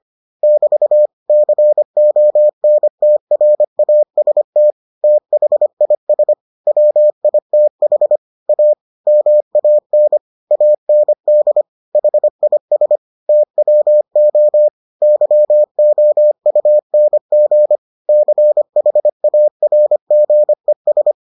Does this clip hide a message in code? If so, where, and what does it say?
Yes: Morse code from 0.4 to 21.1 s, heard as a keyed tone, about 25 wpm, 600 Hz. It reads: <BT> CONTRAST THIS WITH A MAN AND HIS TWO YOUNG CHARGES